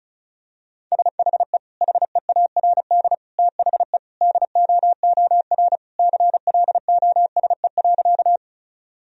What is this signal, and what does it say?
Morse code, heard as a keyed tone, about 36 wpm, 700 Hz. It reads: SHE HEARD THE DOOR CLOSE.
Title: Sentences from Top 1000 Words - Review Part 1 - 35wpm